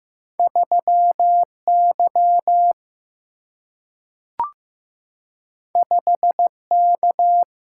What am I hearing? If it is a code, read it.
Morse code, heard as a keyed tone, about 15 wpm, 700 Hz. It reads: 3 Y 5 K
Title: Single-Letter Single-Number 15wpm